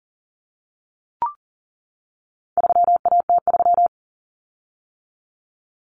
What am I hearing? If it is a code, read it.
Morse code, heard as a keyed tone, about 40 wpm, 700 Hz. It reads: 3AT3
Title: Common Cut Numbers - 4-Digit 40wpm